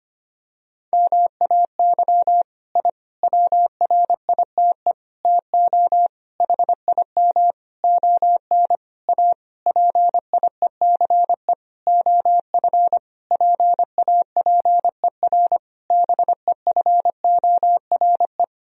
Morse code, heard as a keyed tone, about 25 wpm, 700 Hz. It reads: MAY I WRITE TO HIM ON A PIECE OF PAPER BEFORE
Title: Sentences from Top 1000 Words - Review Part 2 - 25wpm